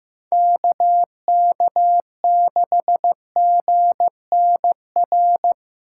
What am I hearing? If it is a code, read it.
Morse code, heard as a keyed tone, about 18 wpm, 700 Hz. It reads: KK6GNR
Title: Speed Racing - USA 2x3 Call Signs 15wpm